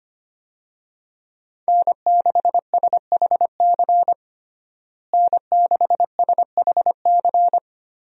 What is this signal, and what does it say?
Morse code, heard as a keyed tone, about 25 wpm, 700 Hz. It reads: N6SHC N6SHC
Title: USA Call Signs - 5 Characters - Repeated 3x - 25wpm